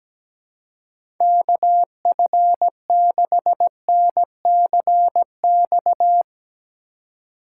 Morse code, heard as a keyed tone, about 17 wpm, 700 Hz. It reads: KF6NCX
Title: K1USN SST QSO Practice - 17wpm